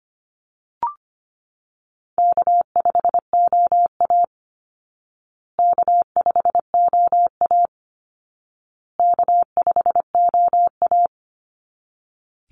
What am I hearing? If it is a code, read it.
Morse code, heard as a keyed tone, about 25 wpm, 700 Hz. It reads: K5OA K5OA K5OA
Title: Call Signs Repeated 2x - 25wpm